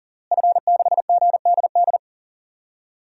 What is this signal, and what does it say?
Morse code, heard as a keyed tone, about 37 wpm, 700 Hz. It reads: F6GDD
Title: DXCC-100 Call Signs - 5 Characters - Repeated 3x - 40wpm